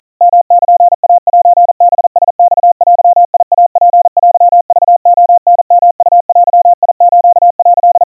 Morse code (Japanese Wabun code), heard as a keyed tone, about 43 wpm, 700 Hz. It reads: ヨルイセハラマデイツテクレタヨウデス」
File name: kawa / Morse sample / kamyu 150 cpm 700 Hz Crispy